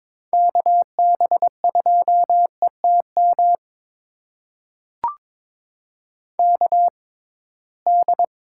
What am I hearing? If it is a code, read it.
Morse code, heard as a keyed tone, about 22 wpm, spaced as for 17 wpm, 700 Hz. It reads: KB2ETM KD
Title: USA Call Signs - 6 Characters - 6x Character Spacing - 22wpm